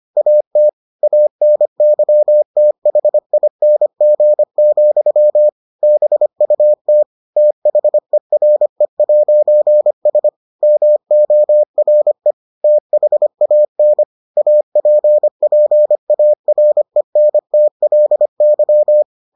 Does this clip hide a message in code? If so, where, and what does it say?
Yes: Morse code from 0.2 to 19.0 s, heard as a keyed tone, about 25 wpm, 600 Hz. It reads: AT ANYTHING, BUT THERE'S MORE THAN APPARENTLY